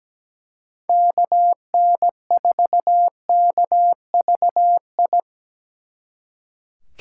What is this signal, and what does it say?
Morse code, heard as a keyed tone, about 20 wpm, 700 Hz. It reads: KN4KVI
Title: Speed Racing - USA 2x3 Call Signs 17wpm